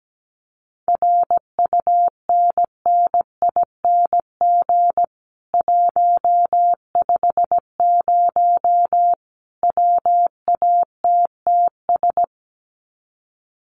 Morse code, heard as a keyed tone, about 17 wpm, 700 Hz. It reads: RUNNING 150 WATTS
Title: Common QSO Phrases 17wpm